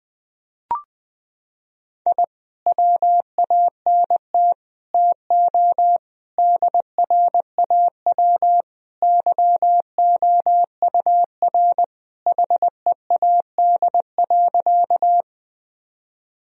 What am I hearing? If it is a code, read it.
Morse code, heard as a keyed tone, about 20 wpm, 700 Hz. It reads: I WANT TO DRAW YOUR HEAD.
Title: Sentences from Top 1000 Words - Review Part 1 - 20wpm